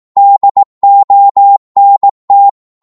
Morse code, heard as a keyed tone, about 18 wpm, 800 Hz. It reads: DONT
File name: Morse / psg-18wpm